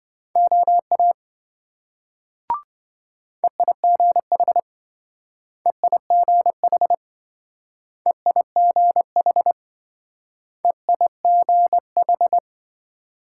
Morse code, heard as a keyed tone, about 26 wpm, 700 Hz. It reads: OA EIGH EIGH EIGH EIGH